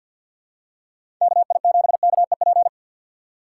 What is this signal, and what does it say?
Morse code, heard as a keyed tone, about 53 wpm, 700 Hz. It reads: KI7KEP
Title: USA Call Signs - 6 Characters - 6x Character Spacing - 50wpm